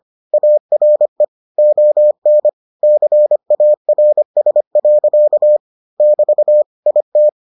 Morse code, heard as a keyed tone, about 25 wpm, 600 Hz. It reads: ARE ON CARS. <BT> IT